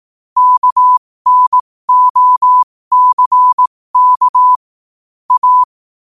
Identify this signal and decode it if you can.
Morse code, heard as a keyed tone, about 18 wpm, spaced as for 15 wpm, 1000 Hz. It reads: KNOCK A
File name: qotdfiles / 15wpm